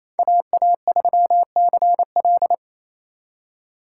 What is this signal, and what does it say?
Morse code, heard as a keyed tone, about 28 wpm, 700 Hz. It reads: AA3CL
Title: USA Call Signs - 5 Characters - 8x Character Spacing - 28wpm